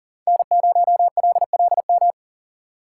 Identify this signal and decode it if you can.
Morse code, heard as a keyed tone, about 50 wpm, 700 Hz. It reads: N0LLM
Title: Speed Racing - USA 1x3 Call Signs 40wpm